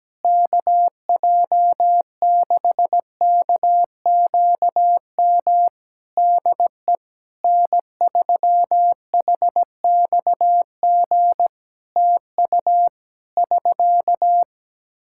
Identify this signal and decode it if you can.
Morse code, heard as a keyed tone, about 17 wpm, 700 Hz. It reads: KJ6KQM DE N3HXG TU <SK>
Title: Practice QSO - Exchange 4 - Ending - 17wpm